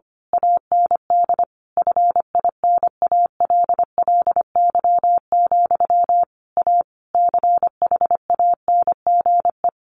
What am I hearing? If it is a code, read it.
Morse code, heard as a keyed tone, about 25 wpm, 700 Hz. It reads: AND FINALLY, A CHANGE
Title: A Sound Of Thunder 25wpm